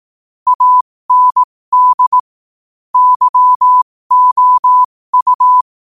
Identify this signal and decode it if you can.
Morse code, heard as a keyed tone, about 18 wpm, spaced as for 15 wpm, 1000 Hz. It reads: AND YOU
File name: qotdfiles / 15wpm